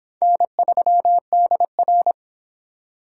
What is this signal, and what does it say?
Morse code, heard as a keyed tone, about 25 wpm, 700 Hz. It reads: N3DR